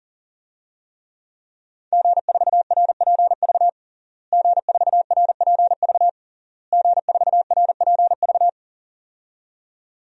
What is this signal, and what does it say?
Morse code, heard as a keyed tone, about 40 wpm, 700 Hz. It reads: G4RPV G4RPV G4RPV
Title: DXCC-100 Call Signs - 5 Characters - Repeated 3x - 40wpm